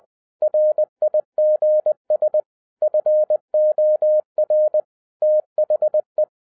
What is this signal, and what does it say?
Morse code, heard as a keyed tone, about 20 wpm, 610 Hz. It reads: RIGS FOR THE